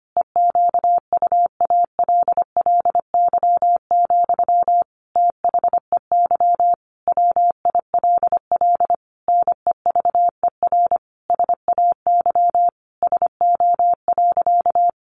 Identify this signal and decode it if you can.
Morse code, heard as a keyed tone, about 25 wpm, 700 Hz. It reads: EQUALLY, THEY WILL NEVER SAY SO.
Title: Sentences from Top 1400 Words 25wpm